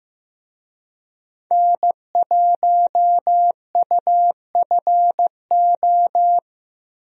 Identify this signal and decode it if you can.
Morse code, heard as a keyed tone, about 15 wpm, 700 Hz. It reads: N1UFO
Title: USA Call Signs - 5 Characters - 15wpm